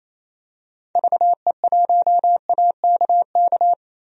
Morse code, heard as a keyed tone, about 27 wpm, 700 Hz. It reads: VE1AKK
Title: DXCC-100 Call Signs - 6 Characters - 28wpm